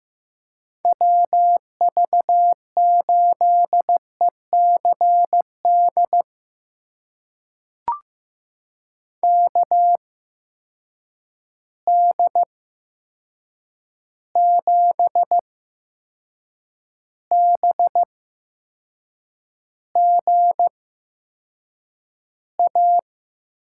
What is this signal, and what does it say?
Morse code, heard as a keyed tone, about 15 wpm, spaced as for 6 wpm, 700 Hz. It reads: WV8ECD KD7BGA
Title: USA Call Signs - 6 Characters - 8x Character Spacing - 15wpm